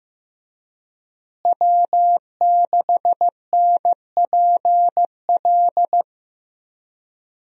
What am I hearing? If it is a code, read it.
Morse code, heard as a keyed tone, about 15 wpm, 700 Hz. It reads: W6NPL
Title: USA Call Signs - 5 Characters - Repeated 3x - 15wpm